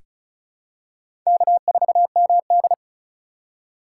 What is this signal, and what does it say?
Morse code, heard as a keyed tone, about 34 wpm, 690 Hz. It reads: K4MD
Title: USA Call Signs - 4 Characters - 2x Character Spacing - 35wpm